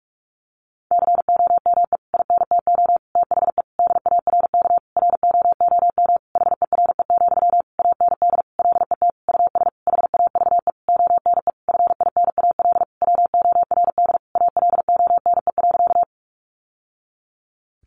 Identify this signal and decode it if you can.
Morse code, heard as a keyed tone, about 44 wpm, 700 Hz. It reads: COME INTO THE DARK ROOM HERE, AND LET US HAVE ONE FINAL WORD ALONE.
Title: Sentences from Top 1000 Words 45wpm